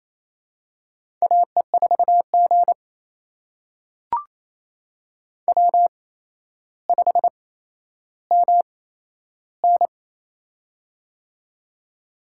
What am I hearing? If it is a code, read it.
Morse code, heard as a keyed tone, about 27 wpm, spaced as for 12 wpm, 700 Hz. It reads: AE4G W5MN
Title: USA Call Signs - 4 Characters - 8x Character Spacing - 28wpm